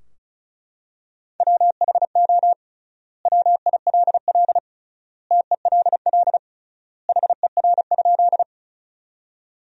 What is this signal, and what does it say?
Morse code, heard as a keyed tone, about 34 wpm, 700 Hz. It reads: WHO WILL TELL HER?
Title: 3x Word-Spacing - Sentences from Top 200 Words - 35wpm